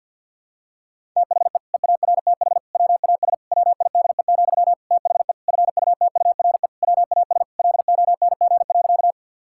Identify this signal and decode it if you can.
Morse code, heard as a keyed tone, about 52 wpm, 700 Hz. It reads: THE EARTH WAS WIDE, THE FUTURE WAS LONG.
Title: Sentences from Top 1000 Words - Review Part 6 - 50wpm